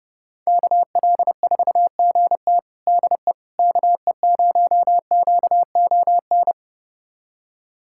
Morse code, heard as a keyed tone, about 32 wpm, 700 Hz. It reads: KL4GT DE KE0QON
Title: USA Callsign DE Callsign 30wpm